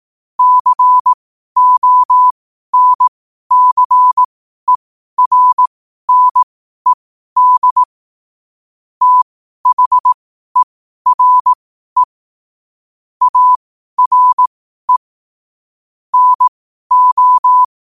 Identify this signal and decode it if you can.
Morse code, heard as a keyed tone, about 18 wpm, spaced as for 12 wpm, 1000 Hz. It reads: CONCERNED THERE ARE NO
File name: qotdfiles / 12wpm